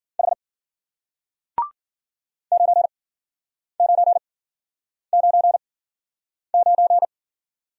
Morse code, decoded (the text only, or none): H 9999